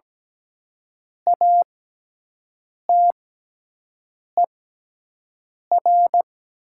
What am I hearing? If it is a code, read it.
Morse code, heard as a keyed tone, about 17 wpm, spaced as for 6 wpm, 700 Hz. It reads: ATER
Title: 6x Character Spacing - Top 100 Words - 17wpm